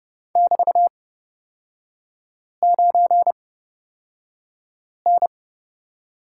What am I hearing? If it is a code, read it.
Morse code, heard as a keyed tone, about 31 wpm, spaced as for 5 wpm, 700 Hz. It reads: <BT>9N
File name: Lesson 36e - BT - Three Characters 30wpm